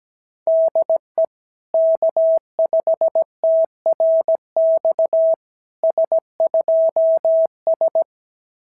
Morse code, heard as a keyed tone, about 17 wpm, 650 Hz. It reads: DE K5TRX S2S
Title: SOTA QSO Practice 17wpm